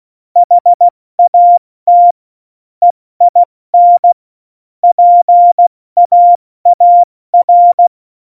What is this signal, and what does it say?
Morse code, heard as a keyed tone, about 16 wpm, 700 Hz. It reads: HAT EIN PAAR
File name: Morsekurs / Witze / Witz 22. August 17, Tempo 70 F80